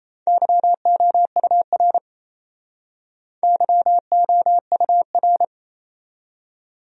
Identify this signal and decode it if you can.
Morse code, heard as a keyed tone, about 37 wpm, 700 Hz. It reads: YOUR YOUR